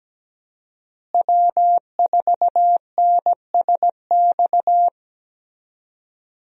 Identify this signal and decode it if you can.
Morse code, heard as a keyed tone, about 17 wpm, 700 Hz. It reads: W4NSX
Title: USA Call Signs - 5 Characters - Repeated 3x - 17wpm